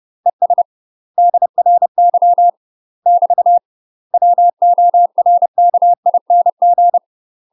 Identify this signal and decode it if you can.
Morse code, heard as a keyed tone, about 30 wpm, 700 Hz. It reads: ES DRY <BT> WORKING
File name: UK MASTER QSOs TRAINING/30WPM